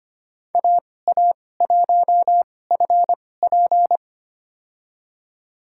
Morse code, heard as a keyed tone, about 25 wpm, spaced as for 18 wpm, 700 Hz. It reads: AA1FP
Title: USA Call Signs - 2x Character Spacing - 25wpm